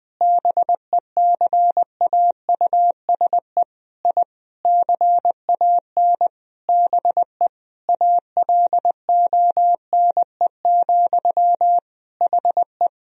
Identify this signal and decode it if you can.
Morse code, heard as a keyed tone, about 20 wpm, 700 Hz. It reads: BECAUSE I CAN BE ALONE, HE
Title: Sentences from Top 1000 Words - Review Part 3 - 20wpm